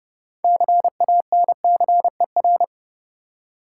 Morse code, heard as a keyed tone, about 30 wpm, 700 Hz. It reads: CANCER
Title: Six Letter Words - 30wpm